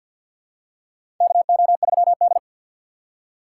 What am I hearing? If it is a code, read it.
Morse code, heard as a keyed tone, about 51 wpm, 700 Hz. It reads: KO3D
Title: USA Call Signs - 4 Characters - 2x Character Spacing - 50wpm